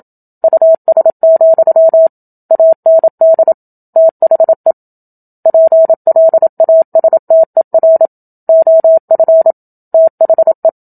Morse code, heard as a keyed tone, about 27 wpm, 650 Hz. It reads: US, AND THE PLASTER OF THE
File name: Book Two / 01-Chapter 1 - 27 WPM